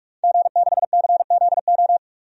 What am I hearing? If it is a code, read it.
Morse code, heard as a keyed tone, about 43 wpm, 700 Hz. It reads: G6CZO
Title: DXCC-100 Call Signs - 5 Characters - 45wpm